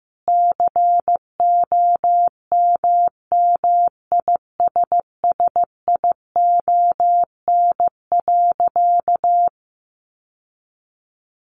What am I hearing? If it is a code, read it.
Morse code, heard as a keyed tone, about 15 wpm, 700 Hz. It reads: COMMISSION.
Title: Sentences from Top 1400 Words 15wpm